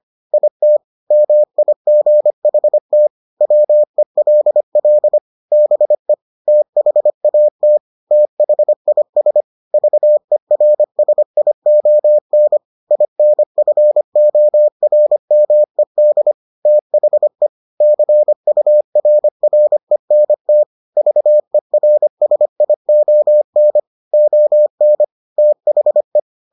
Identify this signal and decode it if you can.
Morse code, heard as a keyed tone, about 25 wpm, 600 Hz. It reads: IT MIGHT WELL BE THAT THIS VERSION INFORMED THE CURRENT VERSION ON THE